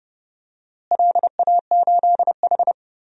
Morse code, heard as a keyed tone, about 30 wpm, 700 Hz. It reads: LA8H
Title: DXCC-100 Call Signs - 4 Characters - Repeated 2x - 30wpm